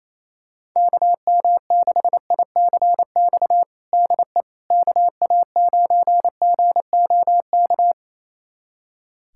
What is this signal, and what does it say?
Morse code, heard as a keyed tone, about 28 wpm, 700 Hz. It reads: KM6ICX DE KA9GOK
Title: USA Callsign DE Callsign - 6 Characters - 28wpm